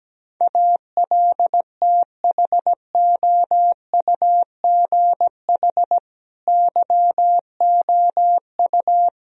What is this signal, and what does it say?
Morse code, heard as a keyed tone, about 17 wpm, 700 Hz. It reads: ALTHOUGH YOU
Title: Sentences from Top 1000 Words - Review Part 3 - 17wpm